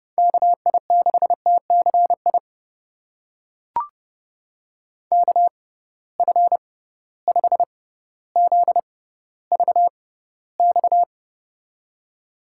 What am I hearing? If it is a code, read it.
Morse code, heard as a keyed tone, about 30 wpm, spaced as for 15 wpm, 700 Hz. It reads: KI6TCI KF5ZVX
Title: USA Call Signs - 6 Characters - 6x Character Spacing - 30wpm